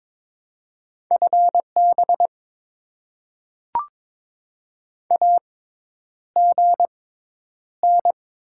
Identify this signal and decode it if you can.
Morse code, heard as a keyed tone, about 22 wpm, spaced as for 10 wpm, 690 Hz. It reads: FB AGN
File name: Top 20 QSO Elements - 6x Character Spacing - 22wpm